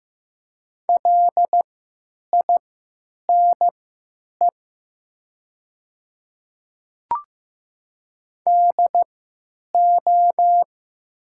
Morse code, heard as a keyed tone, about 15 wpm, spaced as for 9 wpm, 700 Hz. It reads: LINE DO
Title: 3x Character Spacing - Top 300 Words - 15wpm